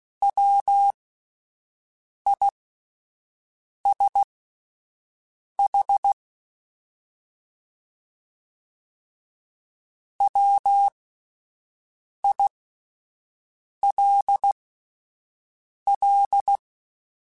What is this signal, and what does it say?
Morse code, heard as a keyed tone, about 16 wpm, spaced as for 5 wpm, 800 Hz. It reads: WISH WILL
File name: Alphabetical / Alphabetical mp3s / Ws @5